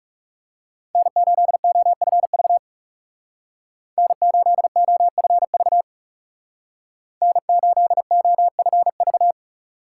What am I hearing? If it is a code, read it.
Morse code, heard as a keyed tone, about 40 wpm, 700 Hz. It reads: N8OFV N8OFV N8OFV